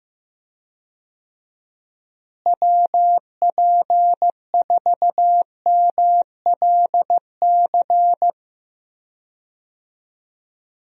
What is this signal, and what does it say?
Morse code, heard as a keyed tone, about 16 wpm, 700 Hz. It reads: WP4MLC
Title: USA Call Signs - 6 Characters - Repeated 2x - 15wpm